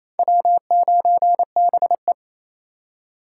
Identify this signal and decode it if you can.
Morse code, heard as a keyed tone, about 37 wpm, 700 Hz. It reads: W9BE